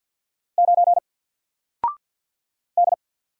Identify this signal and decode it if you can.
Morse code, heard as a keyed tone, about 48 wpm, 700 Hz. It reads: 9 D
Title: Single Letter-Number Alt 50wpm